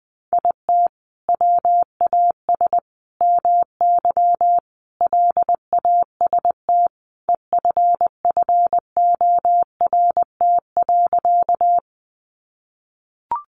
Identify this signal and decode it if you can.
Morse code, heard as a keyed tone, about 20 wpm, 700 Hz. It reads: IT WAS MY LAST EFFORT.
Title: Sentences from Top 1200 Words 20wpm